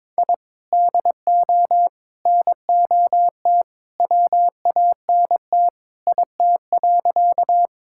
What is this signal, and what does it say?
Morse code, heard as a keyed tone, about 22 wpm, 700 Hz. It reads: I DO NOT WANT IT.
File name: practice / 22wpm / Sentences with 2-Word Phrases - 22wpm